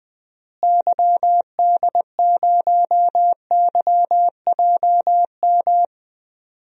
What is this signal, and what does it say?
Morse code, heard as a keyed tone, about 20 wpm, 700 Hz. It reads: YD0YJM
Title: DXCC-100 Call Signs - 6 Characters - Repeated 3x - 20wpm